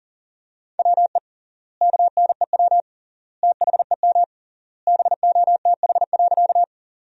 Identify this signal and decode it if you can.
Morse code, heard as a keyed tone, about 40 wpm, 700 Hz. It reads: WE KNEW THEM BOTH.
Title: 3x Word-Spacing - Sentences from Top 400 Words - 40wpm